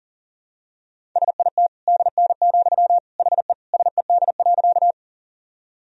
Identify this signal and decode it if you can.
Morse code, heard as a keyed tone, about 40 wpm, 700 Hz. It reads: SIT DN, HE SED.
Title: Sentences from Top 500 Words and Abbreviations 40wpm